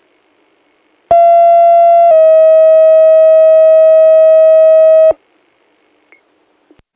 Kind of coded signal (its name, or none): two-tone page